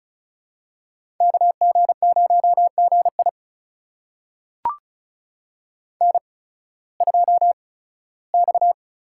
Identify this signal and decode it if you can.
Morse code, heard as a keyed tone, about 36 wpm, spaced as for 19 wpm, 690 Hz. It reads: KG0GI N2X
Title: USA Call Signs - 5 Characters - 8x Character Spacing - 35wpm